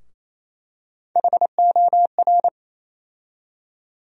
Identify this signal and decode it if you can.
Morse code, heard as a keyed tone, about 27 wpm, 700 Hz. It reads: HOR
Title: USA Call Sign Postfixes - 3 Letters 28wpm